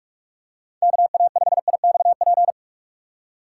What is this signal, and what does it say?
Morse code, heard as a keyed tone, about 44 wpm, 700 Hz. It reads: KA5IXP